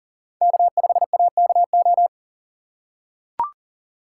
Morse code, heard as a keyed tone, about 40 wpm, 700 Hz. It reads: K5AKO
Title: USA Call Signs - 5 Characters - 2x Character Spacing - 40wpm